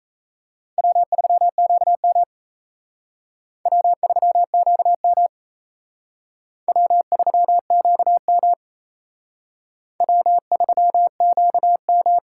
Morse code, heard as a keyed tone, about 35 wpm, 700 Hz. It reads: W3QM W3QM W3QM W3QM
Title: Speed Racing - USA Call Signs 28wpm